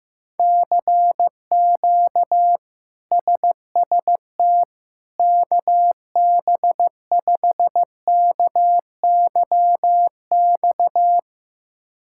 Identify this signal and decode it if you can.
Morse code, heard as a keyed tone, about 15 wpm, 700 Hz. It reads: CQ SST KB5KYX
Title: K1USN SST QSO Practice - 15wpm